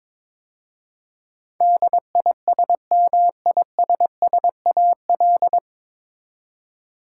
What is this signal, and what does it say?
Morse code, heard as a keyed tone, about 22 wpm, 700 Hz. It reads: DISMISSAL
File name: 9 Letter Words - First 3 Letters Given - 22wpm